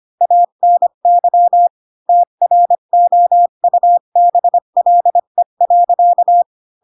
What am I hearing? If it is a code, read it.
Morse code, heard as a keyed tone, about 25 wpm, 700 Hz. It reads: ANY TROUBLE.
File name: Pinocchio / Pinocchio ch3 20 25 ik1hge